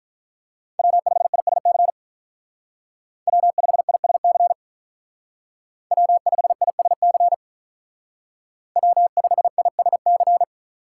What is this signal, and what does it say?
Morse code, heard as a keyed tone, about 44 wpm, 700 Hz. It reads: W5ISC W5ISC W5ISC W5ISC